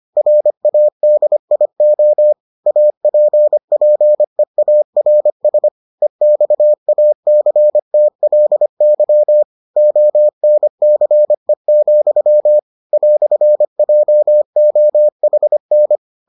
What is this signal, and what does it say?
Morse code, heard as a keyed tone, about 25 wpm, 600 Hz. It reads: RADIO APPEARS EXACTLY ONCE, "JOHN